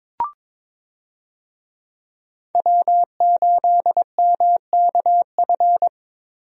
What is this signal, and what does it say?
Morse code, heard as a keyed tone, about 21 wpm, 700 Hz. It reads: W8MKF
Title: USA Call Signs - 5 Characters - Repeated 2x - 22wpm